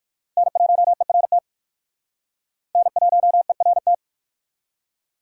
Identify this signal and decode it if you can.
Morse code, heard as a keyed tone, about 54 wpm, 700 Hz. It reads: N1ERT N1ERT
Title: Speed Racing - USA Call Signs 45wpm